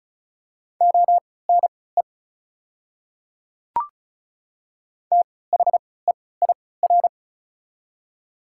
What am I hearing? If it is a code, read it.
Morse code, heard as a keyed tone, about 36 wpm, spaced as for 20 wpm, 690 Hz. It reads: ONE THEIR
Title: 3x Character Spacing - Top 100 Words - 35wpm